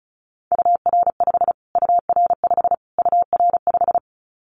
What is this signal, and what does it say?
Morse code, heard as a keyed tone, about 34 wpm, 700 Hz. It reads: UR5 UR5 UR5
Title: Callsign Prefixes for Top 25 DXCC - Repeated 3x - 35wpm